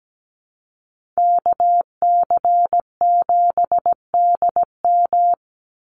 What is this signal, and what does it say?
Morse code, heard as a keyed tone, about 17 wpm, 700 Hz. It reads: KC7DM
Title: NA SOTA Activator Callsigns 17wpm